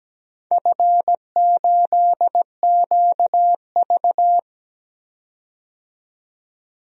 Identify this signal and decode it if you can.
Morse code, heard as a keyed tone, about 17 wpm, 700 Hz. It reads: F8QV
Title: DXCC-100 Call Signs - 4 Characters - 17wpm